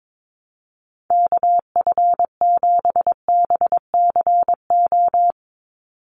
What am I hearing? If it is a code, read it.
Morse code, heard as a keyed tone, about 22 wpm, 700 Hz. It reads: KF7BCO
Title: USA Call Signs 22wpm